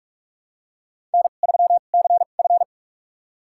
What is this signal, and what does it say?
Morse code, heard as a keyed tone, about 46 wpm, spaced as for 33 wpm, 700 Hz. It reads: N3CF
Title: USA Call Signs - 2x Character Spacing - 45wpm